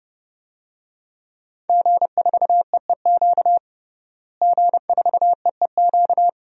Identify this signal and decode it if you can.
Morse code, heard as a keyed tone, about 30 wpm, 700 Hz. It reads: G4EEQ G4EEQ
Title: DXCC-100 Call Signs - 5 Characters - Repeated 2x - 30wpm